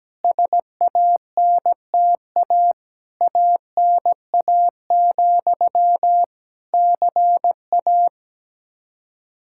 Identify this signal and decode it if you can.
Morse code, heard as a keyed tone, about 17 wpm, 700 Hz. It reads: SANTA ANA, CA
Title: Practice QSO - Exchange 1 - Beginning - 17wpm